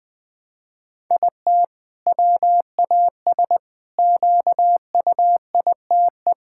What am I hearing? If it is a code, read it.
Morse code, heard as a keyed tone, about 21 wpm, 700 Hz. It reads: IT WAS QUITE
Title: Sentences from Top 1000 Words - Review Part 5 - 20wpm